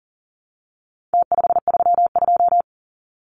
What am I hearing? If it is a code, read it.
Morse code, heard as a keyed tone, about 40 wpm, 700 Hz. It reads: T532
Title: Numbers - Contest Style - 4-Digit 40wpm